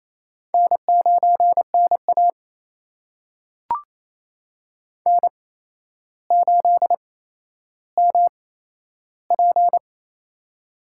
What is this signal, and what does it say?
Morse code, heard as a keyed tone, about 29 wpm, spaced as for 12 wpm, 700 Hz. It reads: N9NA N8MP
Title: USA Call Signs - 4 Characters - 8x Character Spacing - 28wpm